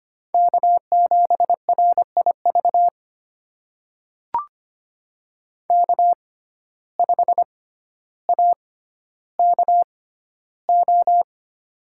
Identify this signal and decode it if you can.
Morse code, heard as a keyed tone, about 25 wpm, spaced as for 13 wpm, 700 Hz. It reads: K7RIV K5AKO
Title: USA Call Signs - 5 Characters - 6x Character Spacing - 25wpm